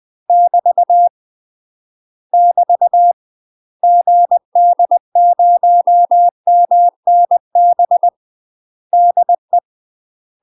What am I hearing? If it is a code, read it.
Morse code, heard as a keyed tone, about 20 wpm, 700 Hz. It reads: <BT> <BT> GD0MNB DE
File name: UK MASTER QSOs TRAINING / 20WPM / UK MASTER QSOs 0025